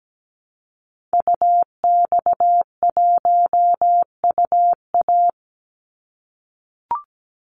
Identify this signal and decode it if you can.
Morse code, heard as a keyed tone, about 17 wpm, 700 Hz. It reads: UX1UA
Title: Call Signs - 17wpm